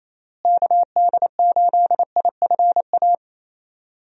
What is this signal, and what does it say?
Morse code, heard as a keyed tone, about 28 wpm, 700 Hz. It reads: KD8IFA
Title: USA Call Signs - 6 Characters - Repeated 3x - 28wpm